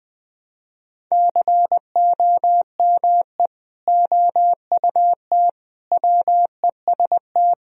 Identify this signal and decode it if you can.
Morse code, heard as a keyed tone, about 20 wpm, 700 Hz. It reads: COME OUT WEST
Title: Sentences from Top 1000 Words - Review Part 2 - 20wpm